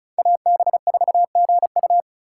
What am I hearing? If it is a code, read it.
Morse code, heard as a keyed tone, about 36 wpm, 700 Hz. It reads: AB4GU
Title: USA Call Signs - 5 Characters - 3x Character Spacing - 35wpm